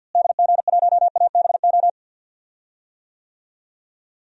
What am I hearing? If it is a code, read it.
Morse code, heard as a keyed tone, about 46 wpm, 700 Hz. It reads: DG1ABO
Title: DXCC-100 Call Signs - 6 Characters - Repeated 3x - 50wpm